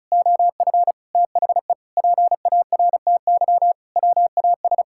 Morse code, heard as a keyed tone, about 36 wpm, 700 Hz. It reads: OF THE PARTY WAS
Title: Sentences from Top 1600 Words 35wpm